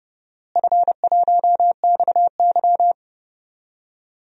Morse code, heard as a keyed tone, about 30 wpm, 700 Hz. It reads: F1XY
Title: DXCC-100 Call Signs - 4 Characters - Repeated 2x - 30wpm